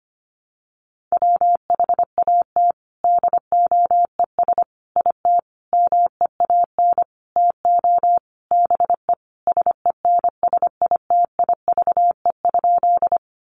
Morse code, heard as a keyed tone, about 25 wpm, 700 Hz. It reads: WHAT DOES IT MEAN TO BE SENSITIVE?
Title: Sentences from Top 1400 Words 25wpm